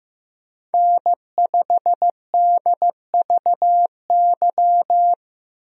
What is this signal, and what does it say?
Morse code, heard as a keyed tone, about 15 wpm, 700 Hz. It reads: N5DVY
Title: USA Call Signs - 5 Characters - 15wpm